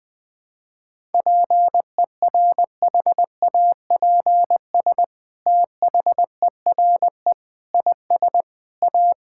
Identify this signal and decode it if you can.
Morse code, heard as a keyed tone, about 20 wpm, 700 Hz. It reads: PERHAPS THERE IS A
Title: Sentences from Top 1000 Words - Review Part 5 - 20wpm